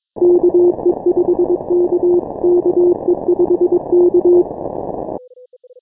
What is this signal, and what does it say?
Morse code, heard as a keyed tone, about 23 wpm, 350 Hz. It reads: KE5K KE5K
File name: final / W1UJ-168-140-MorseRunner